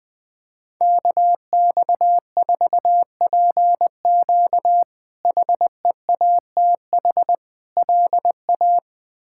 Morse code, heard as a keyed tone, about 20 wpm, 700 Hz. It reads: KX4PQ HEATH LA
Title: K1USN SST QSO Practice - 20wpm